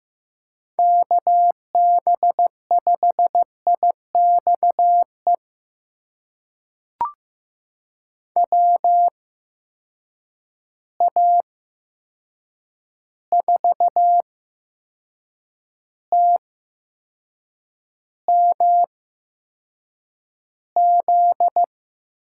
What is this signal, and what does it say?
Morse code, heard as a keyed tone, about 15 wpm, spaced as for 6 wpm, 700 Hz. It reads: KB5IXE WA4TMZ
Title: USA Call Signs - 6 Characters - 8x Character Spacing - 15wpm